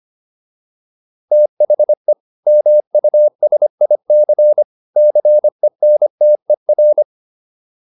Morse code, heard as a keyed tone, about 25 wpm, 600 Hz. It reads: THE MUSIC CENTER
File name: PR403 25